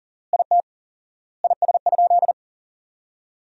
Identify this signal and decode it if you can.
Morse code, heard as a keyed tone, about 40 wpm, 700 Hz. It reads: IT IS?
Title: 4x Word-Spacing - Sentences from Top 100 Words - 40wpm